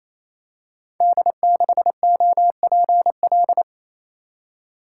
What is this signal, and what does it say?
Morse code, heard as a keyed tone, about 29 wpm, 700 Hz. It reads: D6OPL
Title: Code Groups - 5 Letter:Numbers - 28wpm